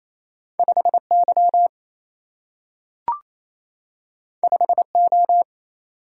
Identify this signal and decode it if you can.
Morse code, heard as a keyed tone, about 28 wpm, 700 Hz. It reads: 5Y 5O
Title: Single-Letter Single-Number 28wpm